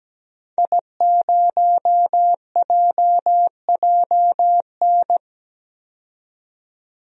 Morse code, heard as a keyed tone, about 17 wpm, 700 Hz. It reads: I0JJN
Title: DXCC-100 Call Signs - 5 Characters - 17wpm